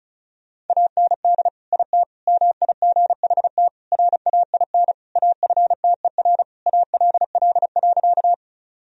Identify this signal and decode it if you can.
Morse code, heard as a keyed tone, about 36 wpm, 700 Hz. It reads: AND IT MIGHT RAIN AFTER ALL.
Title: Sentences from Top 1000 Words - Review Part 1 - 35wpm